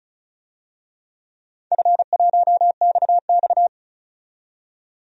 Morse code, heard as a keyed tone, about 34 wpm, 700 Hz. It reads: F1XX
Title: DXCC-100 Call Signs - 4 Characters - 35wpm